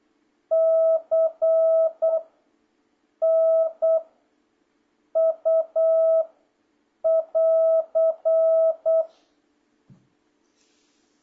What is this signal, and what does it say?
Morse code, heard as a keyed tone, about 8 wpm, 660 Hz. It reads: C N U <AR>